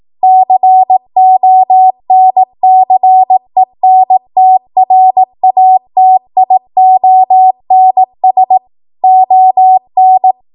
Morse code, heard as a keyed tone, about 18 wpm, 750 Hz. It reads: CONCENTRATIONS ON
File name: Morse / 18WPM